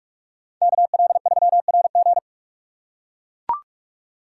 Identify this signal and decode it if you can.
Morse code, heard as a keyed tone, about 44 wpm, 700 Hz. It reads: KL3RG